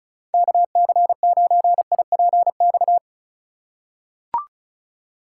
Morse code, heard as a keyed tone, about 34 wpm, 700 Hz. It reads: KC9IPX
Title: USA Call Signs - 6 Characters - 3x Character Spacing - 35wpm